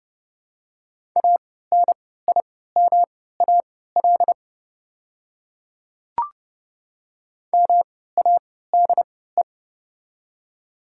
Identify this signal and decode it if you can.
Morse code, heard as a keyed tone, about 30 wpm, spaced as for 17 wpm, 700 Hz. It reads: ANIMAL MADE
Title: 3x Character Spacing - Top 300 Words - 30wpm